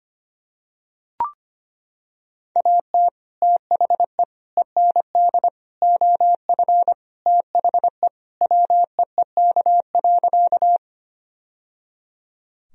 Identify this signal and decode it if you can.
Morse code, heard as a keyed tone, about 25 wpm, 700 Hz. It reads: AT THE END OF THE WEEK.
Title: Sentences from Top 1000 Words - Review Part 2 - 25wpm